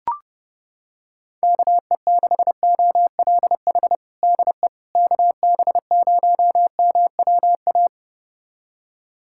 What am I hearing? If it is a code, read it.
Morse code, heard as a keyed tone, about 30 wpm, 700 Hz. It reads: KE6OLH DE KB0MWA
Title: USA Callsign DE Callsign - 6 Characters - 30wpm